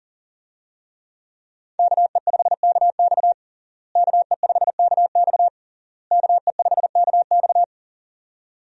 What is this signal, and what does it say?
Morse code, heard as a keyed tone, about 40 wpm, 700 Hz. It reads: KE5KX KE5KX KE5KX
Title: USA Call Signs - 5 Characters - Repeated 3x - 40wpm